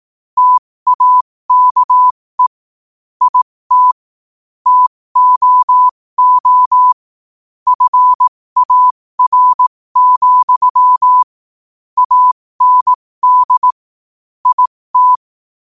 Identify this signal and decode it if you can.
Morse code, heard as a keyed tone, about 18 wpm, spaced as for 15 wpm, 1000 Hz. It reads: TAKE IT TOO FAR, AND IT
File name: qotdfiles / 15wpm